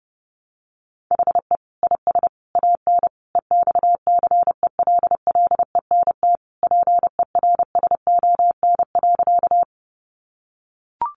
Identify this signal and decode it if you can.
Morse code, heard as a keyed tone, about 30 wpm, 700 Hz. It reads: HE IS AN EXCELLENT PERSON.
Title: Sentences from Top 1000 Words 30wpm